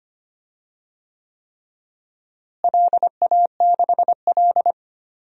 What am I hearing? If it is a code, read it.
Morse code, heard as a keyed tone, about 25 wpm, 700 Hz. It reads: LA6L
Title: DXCC-100 Call Signs - 4 Characters - Repeated 3x - 25wpm